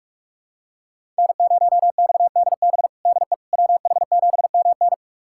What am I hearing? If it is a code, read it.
Morse code, heard as a keyed tone, about 47 wpm, 700 Hz. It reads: N0XDB DE WH7MN
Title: USA Callsign DE Callsign - 5 Characters - 45wpm